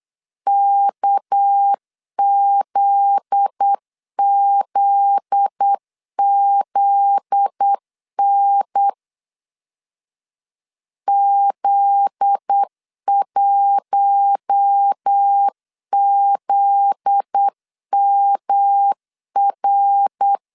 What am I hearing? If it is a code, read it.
Morse code, heard as a keyed tone, about 8 wpm, 800 Hz. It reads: KZZZN Z1ZMR
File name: LEZIONI / 7 GF9Z / 7 GF9Z 8wpm 3213